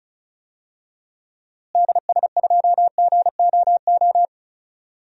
Morse code, heard as a keyed tone, about 34 wpm, 700 Hz. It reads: DS2GOO
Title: DXCC-100 Call Signs - 6 Characters - Repeated 3x - 35wpm